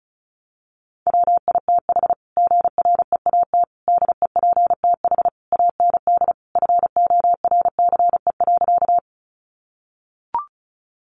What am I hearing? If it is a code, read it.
Morse code, heard as a keyed tone, about 34 wpm, 700 Hz. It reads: WITH GREAT DEPTH AND FORCE.
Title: Sentences from Top 1200 Words 35wpm